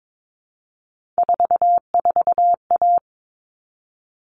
Morse code, heard as a keyed tone, about 22 wpm, 700 Hz. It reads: 44A
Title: All Cut Numbers - 3-Digit 22wpm